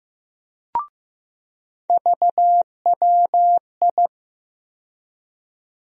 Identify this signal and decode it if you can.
Morse code, heard as a keyed tone, about 15 wpm, 700 Hz. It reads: VWI
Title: USA Call Sign Postfixes - 3 Letters 15wpm